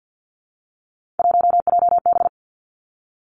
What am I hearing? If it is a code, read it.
Morse code, heard as a keyed tone, about 50 wpm, 700 Hz. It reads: 1JB
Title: Three Challenging Letter-Number Combinations 50wpm